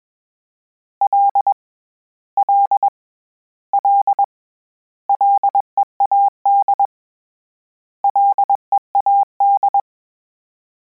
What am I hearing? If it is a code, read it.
Morse code, heard as a keyed tone, about 21 wpm, 800 Hz. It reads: L L L LEAD LEAD
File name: Combinations / Combinations-mp3s / Combinations@20